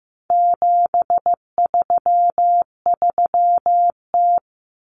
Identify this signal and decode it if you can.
Morse code, heard as a keyed tone, about 15 wpm, 700 Hz. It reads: 733T
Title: Common Cut Numbers - 4-Digit 15wpm